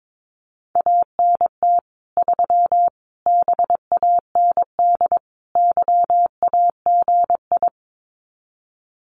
Morse code, heard as a keyed tone, about 22 wpm, 700 Hz. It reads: ANT 3 BAND YAGI
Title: Common QSO Phrases 22wpm